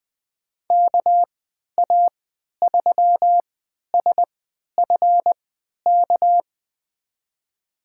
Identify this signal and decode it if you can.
Morse code, heard as a keyed tone, about 20 wpm, spaced as for 11 wpm, 700 Hz. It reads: KA3SFK
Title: USA Call Signs - 6 Characters - 3x Character Spacing - 20wpm